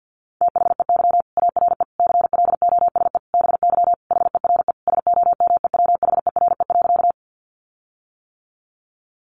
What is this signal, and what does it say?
Morse code, heard as a keyed tone, about 50 wpm, 700 Hz. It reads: THEY ARE CLOSE BY HERE SOMEWHERE.
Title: Sentences from Top 1400 Words 50wpm